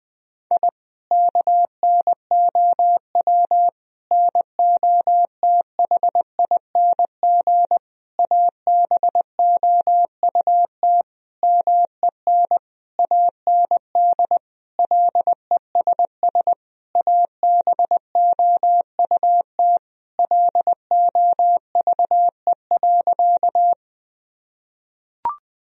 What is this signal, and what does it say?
Morse code, heard as a keyed tone, about 20 wpm, 700 Hz. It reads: I KNOW NOTHING ABOUT MEN AND LESS ABOUT LOVE.
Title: Sentences from Top 1000 Words - Review Part 2 - 20wpm